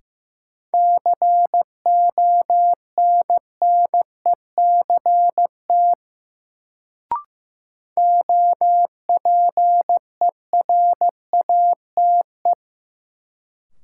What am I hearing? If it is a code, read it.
Morse code, heard as a keyed tone, about 15 wpm, 700 Hz. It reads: CONNECT OPERATE
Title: Sentences from Top 1500 Words - New Word Drill - 15wpm